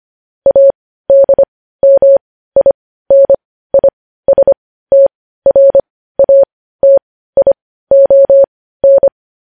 Morse code, heard as a keyed tone, about 25 wpm, spaced as for 15 wpm, 550 Hz. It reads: ADMINISTRATION
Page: news-headlines-in-morse-code-at-15-wpm